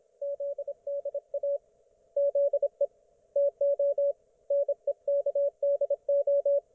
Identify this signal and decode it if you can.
Morse code, heard as a keyed tone, about 26 wpm, 560 Hz. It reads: ZDA ZE TO NEKDO